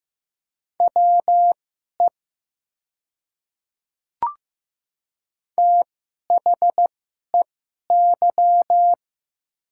Morse code, heard as a keyed tone, about 15 wpm, spaced as for 11 wpm, 700 Hz. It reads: WE THEY
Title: 2x Character Spacing - Top 100 Words - 15wpm